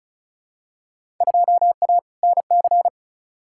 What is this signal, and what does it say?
Morse code, heard as a keyed tone, about 34 wpm, 700 Hz. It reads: 2A NC
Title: Field Day Exchanges - 35wpm